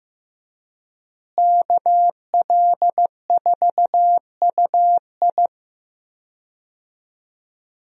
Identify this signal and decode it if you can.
Morse code, heard as a keyed tone, about 15 wpm, 700 Hz. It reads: KL4UI
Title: USA Call Signs - 5 Characters - Repeated 3x - 15wpm